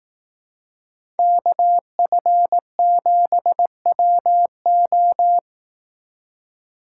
Morse code, heard as a keyed tone, about 18 wpm, 700 Hz. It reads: KF7WO
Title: Speed Racing - USA 2x2 Call Signs 15wpm